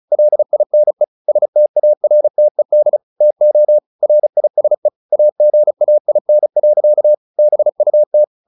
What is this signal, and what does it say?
Morse code, heard as a keyed tone, about 34 wpm, 600 Hz. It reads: LINE STARTED TO RISE AGAIN. BUT
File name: the old man and the sea / the old man and the sea 1 35wpm